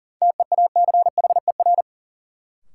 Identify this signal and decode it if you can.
Morse code, heard as a keyed tone, about 37 wpm, 700 Hz. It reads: TEACHER
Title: Sentences from Top 900 Words - New Word Drill - 40wpm